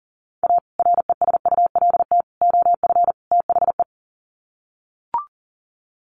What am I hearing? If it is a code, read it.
Morse code, heard as a keyed tone, about 40 wpm, 700 Hz. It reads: A RESULT OF THE
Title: Sets of 4 Words 40wpm